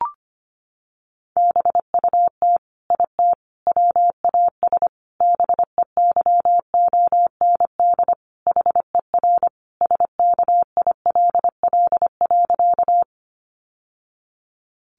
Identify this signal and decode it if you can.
Morse code, heard as a keyed tone, about 25 wpm, 700 Hz. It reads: BUT IT WAS BEYOND HER SKILL.
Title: Sentences from Top 1200 Words 25wpm